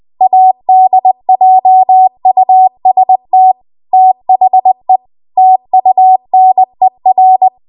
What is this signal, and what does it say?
Morse code, heard as a keyed tone, about 20 wpm, 750 Hz. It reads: ADJUST THE TUNER
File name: Archive / 20 WPM / 20WPM